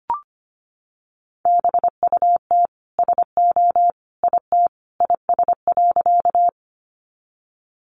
Morse code, heard as a keyed tone, about 25 wpm, 700 Hz. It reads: BUT SO IT IS.
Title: Sentences from Top 100 Words 25wpm